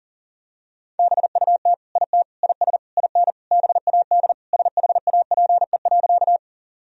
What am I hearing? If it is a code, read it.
Morse code, heard as a keyed tone, about 40 wpm, 700 Hz. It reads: BUT IT IS IN BAD SHAPE.
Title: Sentences from Top 1000 Words - Review Part 3 - 40wpm